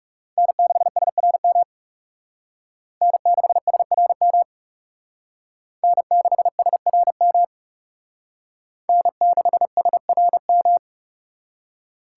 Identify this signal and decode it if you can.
Morse code, heard as a keyed tone, about 36 wpm, 700 Hz. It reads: N6SRM N6SRM N6SRM N6SRM
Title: Speed Racing - USA 1x3 Call Signs 30wpm